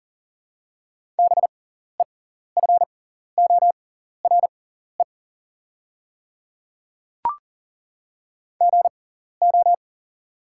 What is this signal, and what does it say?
Morse code, heard as a keyed tone, about 40 wpm, spaced as for 14 wpm, 690 Hz. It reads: BEFORE GO
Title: 6x Character Spacing - Top 200 Words - 40wpm